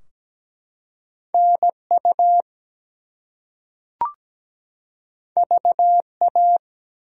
Morse code, heard as a keyed tone, about 17 wpm, 700 Hz. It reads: NU VA
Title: US States and Canadian Provinces 17wpm